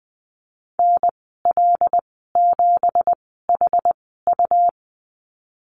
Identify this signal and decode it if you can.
Morse code, heard as a keyed tone, about 20 wpm, spaced as for 14 wpm, 700 Hz. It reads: NL7HU
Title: 179 Lesson 29g - Slash - Call Signs 20wpm